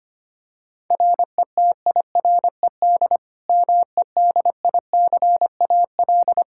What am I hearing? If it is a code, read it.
Morse code, heard as a keyed tone, about 25 wpm, 700 Hz. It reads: RETIRED MEDICAL
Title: Practice QSO - Exchange 3 - More Info - 25wpm